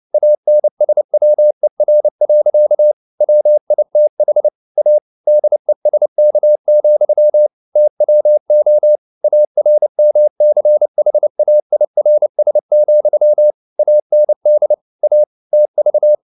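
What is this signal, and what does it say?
Morse code, heard as a keyed tone, about 29 wpm, 600 Hz. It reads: ANSWER. WITH A DESK, TWO ARMCHAIRS, AND A TV
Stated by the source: Morse code at 29 wpm, sending Ken Follett's novel Never in English, from CW med Gnister.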